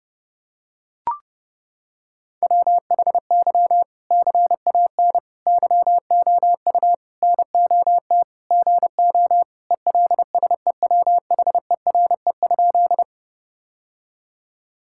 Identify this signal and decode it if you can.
Morse code, heard as a keyed tone, about 30 wpm, 700 Hz. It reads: WHY CAN YOU NOT GO ELSEWHERE?
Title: Sentences from Top 1600 Words 30wpm